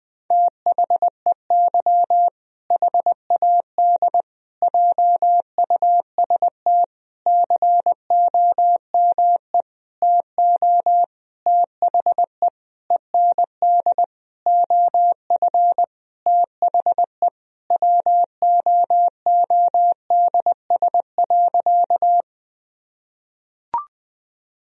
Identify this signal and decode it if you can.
Morse code, heard as a keyed tone, about 20 wpm, 700 Hz. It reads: THEY HAD JUST COME TO THE END OF THE WOODS.
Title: Sentences from Top 1000 Words - Review Part 2 - 20wpm